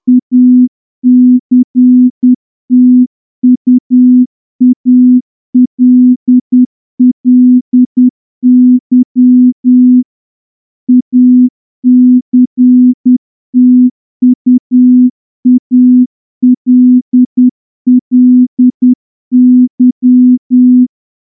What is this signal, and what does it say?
Morse code, heard as a keyed tone, about 10 wpm, 250 Hz. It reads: ACTUALLY ACTUALLY
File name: morse-training-10WPM